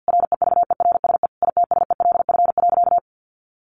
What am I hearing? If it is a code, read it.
Morse code, heard as a keyed tone, about 50 wpm, 700 Hz. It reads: REVERSE ITSELF.
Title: A Sound Of Thunder 50wpm